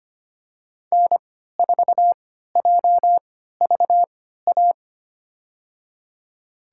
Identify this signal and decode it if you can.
Morse code, heard as a keyed tone, about 25 wpm, spaced as for 14 wpm, 700 Hz. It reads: N4JVA
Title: USA Call Signs - 5 Characters - 3x Character Spacing - 25wpm